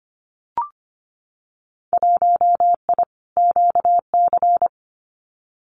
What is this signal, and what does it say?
Morse code, heard as a keyed tone, about 25 wpm, 700 Hz. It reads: 1I QC
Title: Winter Field Day Exchanges 25wpm